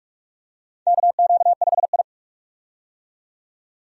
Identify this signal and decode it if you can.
Morse code, heard as a keyed tone, about 46 wpm, 700 Hz. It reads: KQ5I
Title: USA Call Signs - 4 Characters - 45wpm